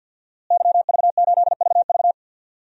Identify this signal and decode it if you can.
Morse code, heard as a keyed tone, about 53 wpm, 700 Hz. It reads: YV8VV
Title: DXCC-100 Call Signs - 5 Characters - Repeated 2x - 50wpm